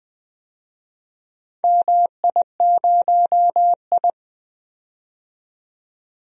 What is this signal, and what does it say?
Morse code, heard as a keyed tone, about 20 wpm, 700 Hz. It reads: MI0I
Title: DXCC-100 Call Signs - 4 Characters - 20wpm